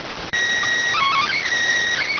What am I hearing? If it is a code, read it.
SSTV (slow-scan television), PD 290.